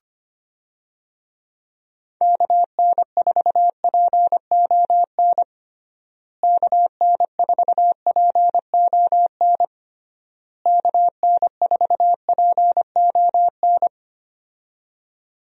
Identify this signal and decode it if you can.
Morse code, heard as a keyed tone, about 25 wpm, 700 Hz. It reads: KN4PON KN4PON KN4PON
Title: USA Call Signs - 6 Characters - Repeated 3x - 25wpm